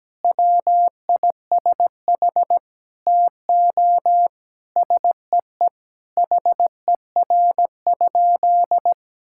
Morse code, heard as a keyed tone, about 17 wpm, 700 Hz. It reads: WISH TO SEE HER?
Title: Sentences from Top 1600 Words 17wpm